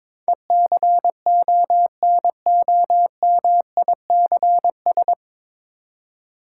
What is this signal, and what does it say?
Morse code, heard as a keyed tone, about 22 wpm, 700 Hz. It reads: ECONOMICS
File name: 9 Letter Words - First 3 Letters Given - 22wpm